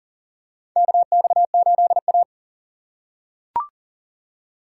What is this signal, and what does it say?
Morse code, heard as a keyed tone, about 38 wpm, 700 Hz. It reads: KX8A
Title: USA Call Signs - 4 Characters - 6x Character Spacing - 40wpm